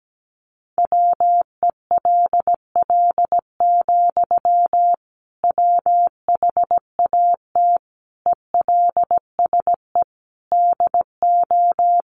Morse code, heard as a keyed tone, about 17 wpm, 700 Hz. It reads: WELL, WHAT ELSE DO
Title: Sentences from Top 800 Words 17wpm